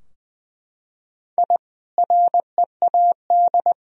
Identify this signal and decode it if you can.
Morse code, heard as a keyed tone, about 21 wpm, 700 Hz. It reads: I READ
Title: Sentences from Top 1000 Words - Review Part 4 - 20wpm